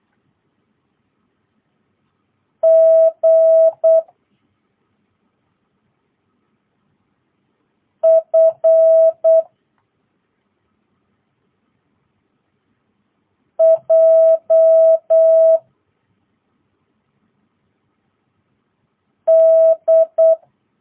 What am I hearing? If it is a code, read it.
Morse code, heard as a keyed tone, about 8 wpm, 650 Hz. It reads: G F J D
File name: jh8cbh / 2022cw